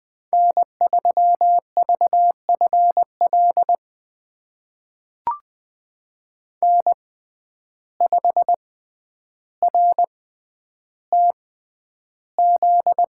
Morse code, heard as a keyed tone, about 20 wpm, spaced as for 10 wpm, 700 Hz. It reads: N3VFL N5RTZ
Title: USA Call Signs - 5 Characters - 6x Character Spacing - 20wpm